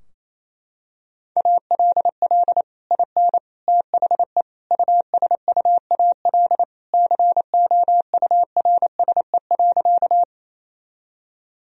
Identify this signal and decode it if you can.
Morse code, heard as a keyed tone, about 27 wpm, 700 Hz. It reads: ALL IN THE USUAL COURSE.
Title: Sentences from Top 1000 Words - Review Part 2 - 28wpm